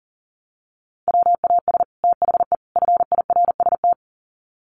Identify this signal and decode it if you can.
Morse code, heard as a keyed tone, about 40 wpm, 700 Hz. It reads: WAS THE FIRST
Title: Sets of 3 Words 40wpm